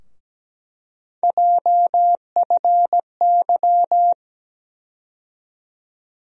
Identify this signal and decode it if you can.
Morse code, heard as a keyed tone, about 17 wpm, 700 Hz. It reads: JFY
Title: USA Call Sign Postfixes - 3 Letters 17wpm